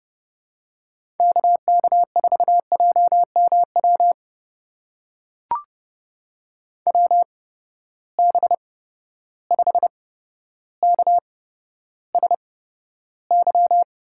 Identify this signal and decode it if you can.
Morse code, heard as a keyed tone, about 30 wpm, spaced as for 13 wpm, 700 Hz. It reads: KK4JMW WB5KSY
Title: USA Call Signs - 6 Characters - 8x Character Spacing - 30wpm